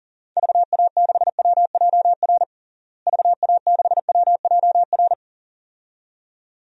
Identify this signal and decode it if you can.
Morse code, heard as a keyed tone, about 40 wpm, 700 Hz. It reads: VA6WJR VA6WJR
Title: DXCC-100 Call Signs - 6 Characters - Repeated 2x - 40wpm